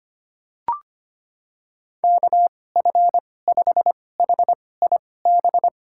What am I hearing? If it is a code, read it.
Morse code, heard as a keyed tone, about 25 wpm, spaced as for 18 wpm, 700 Hz. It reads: KF5HIB
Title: USA Call Signs - 6 Characters - 2x Character Spacing - 25wpm